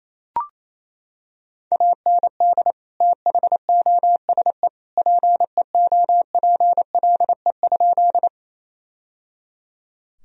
Morse code, heard as a keyed tone, about 28 wpm, 700 Hz. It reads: AND THOSE PEOPLE?
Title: Sentences from Top 1000 Words - Review Part 1 - 28wpm